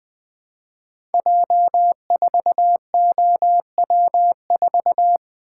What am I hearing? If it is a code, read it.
Morse code, heard as a keyed tone, about 20 wpm, 700 Hz. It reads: J4OW4
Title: Code Groups - 5 Letter:Numbers - 20wpm